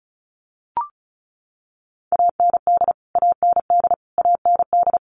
Morse code, heard as a keyed tone, about 34 wpm, 700 Hz. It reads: AND AND AND
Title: Top 500 Words Repeated 35wpm